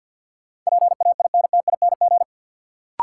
Morse code, heard as a keyed tone, about 48 wpm, 700 Hz. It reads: PAINTING
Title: Sentences from Top 1000 Words - Review Part 6 - New Word Drill - 50wpm